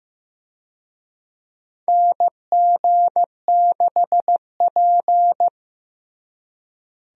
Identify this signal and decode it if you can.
Morse code, heard as a keyed tone, about 15 wpm, 700 Hz. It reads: NG6P
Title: USA Call Signs - 4 Characters - Repeated 3x - 15wpm